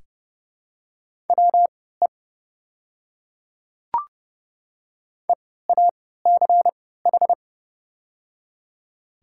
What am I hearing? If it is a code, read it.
Morse code, heard as a keyed tone, about 30 wpm, spaced as for 17 wpm, 700 Hz. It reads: WE EACH
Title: 3x Character Spacing - Top 100 Words - 30wpm